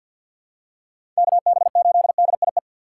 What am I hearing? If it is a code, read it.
Morse code, heard as a keyed tone, about 52 wpm, 690 Hz. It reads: KB8DIE
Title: USA Call Signs - 6x Character Spacing - 50wpm